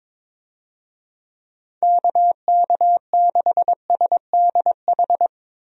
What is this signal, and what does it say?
Morse code, heard as a keyed tone, about 22 wpm, 700 Hz. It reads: KK6SDH
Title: USA Call Signs - 6 Characters - 22wpm